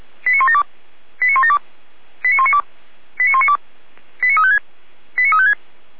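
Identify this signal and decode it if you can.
Five-tone selective call (ZVEI), heard as quick runs of five tones, tones of about 70 ms. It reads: 89172 89172 89192 89192 89356 89356
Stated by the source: ZVEI five-tone selective call received on an SDR for Feuerwehr Orlamünde.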